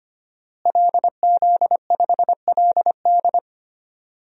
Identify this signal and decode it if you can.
Morse code, heard as a keyed tone, about 25 wpm, 700 Hz. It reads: LZ5LD
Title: DXCC-100 Call Signs - 5 Characters - Repeated 2x - 25wpm